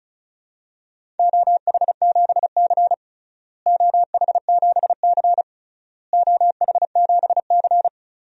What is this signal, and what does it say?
Morse code, heard as a keyed tone, about 34 wpm, 700 Hz. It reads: OH7C OH7C OH7C
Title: DXCC-100 Call Signs - 4 Characters - Repeated 3x - 35wpm